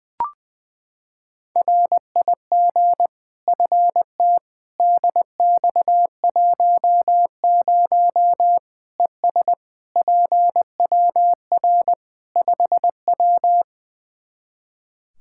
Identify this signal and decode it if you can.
Morse code, heard as a keyed tone, about 21 wpm, 700 Hz. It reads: RIG FT DX10 ES PWR 5W
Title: Practice QSO - Exchange 2 - Info - 20wpm